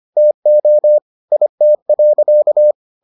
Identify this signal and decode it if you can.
Morse code, heard as a keyed tone, about 25 wpm, 600 Hz. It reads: TO IT.